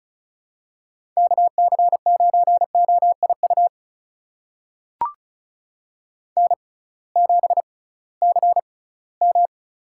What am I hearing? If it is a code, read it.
Morse code, heard as a keyed tone, about 36 wpm, spaced as for 21 wpm, 700 Hz. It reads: KC9OIU N7CM
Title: USA Call Signs - 6x Character Spacing - 35wpm